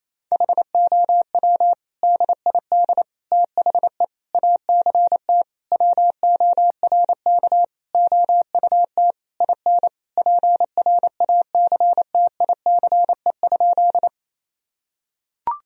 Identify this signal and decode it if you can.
Morse code, heard as a keyed tone, about 28 wpm, 700 Hz. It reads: HOW DID THE ACT WORK OUT IN PRACTICE?
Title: Sentences from Top 1000 Words - Review Part 4 - 28wpm